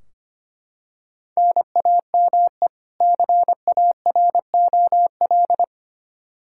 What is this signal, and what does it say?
Morse code, heard as a keyed tone, about 25 wpm, 700 Hz. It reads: NAME CAROL
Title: Practice QSO - Exchange 1 - Beginning - 25wpm